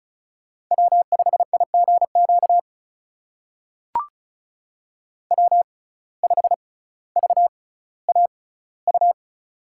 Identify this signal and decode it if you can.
Morse code, heard as a keyed tone, about 36 wpm, spaced as for 18 wpm, 700 Hz. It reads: W5IGQ W5VAU
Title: USA Call Signs - 5 Characters - 6x Character Spacing - 35wpm